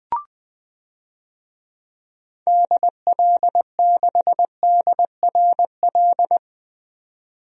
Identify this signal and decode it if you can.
Morse code, heard as a keyed tone, about 20 wpm, 700 Hz. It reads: DL6DRL
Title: DXCC-100 Call Signs - 6 Characters - 20wpm